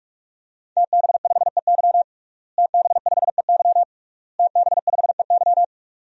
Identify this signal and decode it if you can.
Morse code, heard as a keyed tone, about 46 wpm, 700 Hz. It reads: TB5EY TB5EY TB5EY
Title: DXCC-100 Call Signs - 5 Characters - Repeated 3x - 45wpm